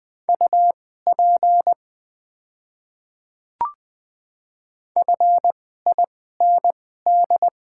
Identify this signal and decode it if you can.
Morse code, heard as a keyed tone, about 20 wpm, spaced as for 15 wpm, 700 Hz. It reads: UP FIND
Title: 2x Character Spacing - Top 100 Words - 20wpm